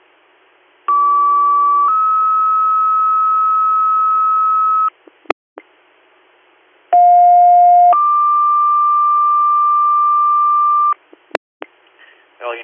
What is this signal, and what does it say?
Two-tone page, heard as a one-second tone, then a three-second tone, about 1150 then 1290 Hz, about 710 then 1180 Hz. It is heard from 0.9 to 10.9 s.